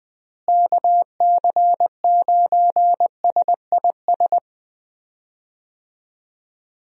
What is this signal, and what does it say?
Morse code, heard as a keyed tone, about 21 wpm, 700 Hz. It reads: KC9SIS
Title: USA Call Signs - 6 Characters - Repeated 2x - 20wpm